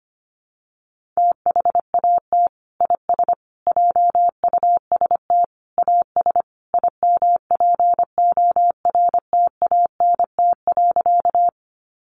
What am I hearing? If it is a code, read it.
Morse code, heard as a keyed tone, about 25 wpm, 700 Hz. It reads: THAT IS JUST AS IMPORTANT.
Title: Sentences from Top 600 Words 25wpm